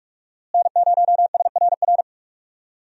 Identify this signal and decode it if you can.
Morse code, heard as a keyed tone, about 44 wpm, 700 Hz. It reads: N0SRR
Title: USA Call Signs - 5 Characters - Repeated 2x - 45wpm